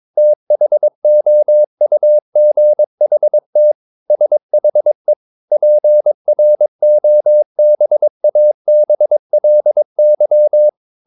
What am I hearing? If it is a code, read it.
Morse code, heard as a keyed tone, about 22 wpm, 600 Hz. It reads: THOUGHT SHE PROBABLY
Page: Romanen Never af Ken Follet på engelsk.